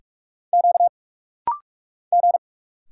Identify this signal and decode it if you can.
Morse code, heard as a keyed tone, about 46 wpm, 700 Hz. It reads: Q G
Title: Single Letters - Rapid-Fire 45wpm